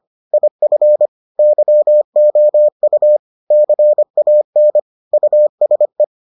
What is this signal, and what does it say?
Morse code, heard as a keyed tone, about 25 wpm, 600 Hz. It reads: IF YOU CAN USE